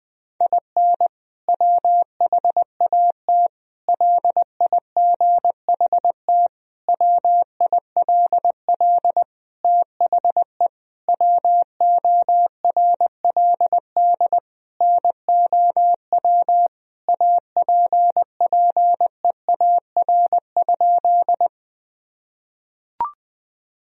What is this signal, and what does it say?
Morse code, heard as a keyed tone, about 20 wpm, 700 Hz. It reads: IN WHAT LIGHT WILL THE WORLD NOW APPEAR?
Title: Sentences from Top 1000 Words - Review Part 2 - 20wpm